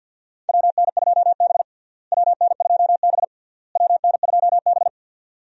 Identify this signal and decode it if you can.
Morse code, heard as a keyed tone, about 52 wpm, 700 Hz. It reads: WN2B WN2B WN2B
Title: USA Call Signs - 4 Characters - Repeated 3x - 50wpm